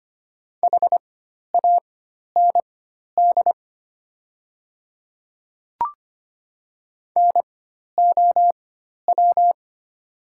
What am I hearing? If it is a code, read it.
Morse code, heard as a keyed tone, about 25 wpm, spaced as for 12 wpm, 700 Hz. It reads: HAND NOW
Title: 4x Character Spacing - Top 200 Words - 25wpm